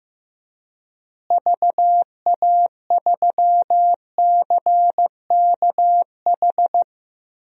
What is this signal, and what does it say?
Morse code, heard as a keyed tone, about 16 wpm, 700 Hz. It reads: VA3CKH
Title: DXCC-100 Call Signs - 6 Characters - Repeated 3x - 15wpm